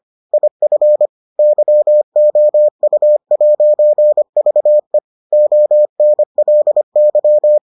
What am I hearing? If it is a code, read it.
Morse code, heard as a keyed tone, about 25 wpm, 600 Hz. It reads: IF YOU'VE ONLY